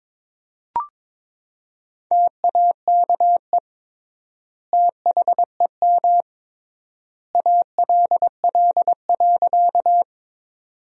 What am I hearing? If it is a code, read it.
Morse code, heard as a keyed tone, about 22 wpm, 700 Hz. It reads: TAKE THEM ALL.
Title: 3x Word-Spacing - Sentences from Top 200 Words - 22wpm